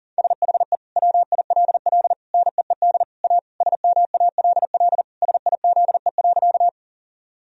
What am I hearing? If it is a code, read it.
Morse code, heard as a keyed tone, about 37 wpm, 700 Hz. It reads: SHE WILL NEED A SMALL SIZE.
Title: Sentences from Top 1000 Words - Review Part 2 - 40wpm